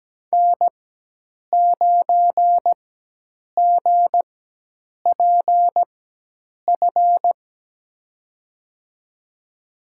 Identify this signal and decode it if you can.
Morse code, heard as a keyed tone, about 17 wpm, spaced as for 8 wpm, 700 Hz. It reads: N9GPF
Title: USA Call Signs - 4x Character Spacing - 17wpm